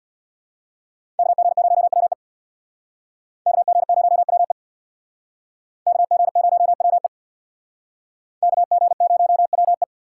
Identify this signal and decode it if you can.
Morse code, heard as a keyed tone, about 62 wpm, 700 Hz. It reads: KG0WE KG0WE KG0WE KG0WE
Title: Speed Racing - USA 2x2 Call Signs 50wpm